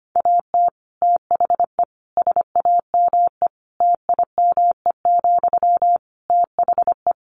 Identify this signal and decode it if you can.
Morse code, heard as a keyed tone, about 25 wpm, 700 Hz. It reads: AT THE SAME TIME, THE
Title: Sentences from Top 1500 Words 25wpm